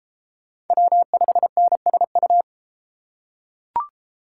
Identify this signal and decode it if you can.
Morse code, heard as a keyed tone, about 32 wpm, 700 Hz. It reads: W5NSU